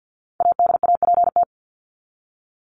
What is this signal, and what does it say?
Morse code, heard as a keyed tone, about 50 wpm, 700 Hz. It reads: ADAPT
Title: Top 1501-1750 Words 50wpm